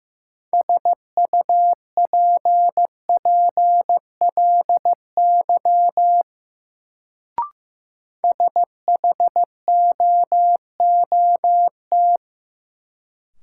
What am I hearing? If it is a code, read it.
Morse code, heard as a keyed tone, about 15 wpm, 700 Hz. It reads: SUPPLY SHOOT
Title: Sentences from Top 1200 Words - New Word Drill Alt - 15wpm